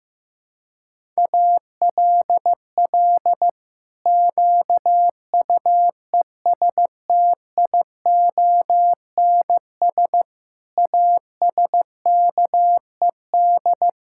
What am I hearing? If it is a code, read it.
Morse code, heard as a keyed tone, about 15 wpm, 700 Hz. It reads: ALL QUESTIONS ASKED
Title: Pangram Challenge v2 - 15wpm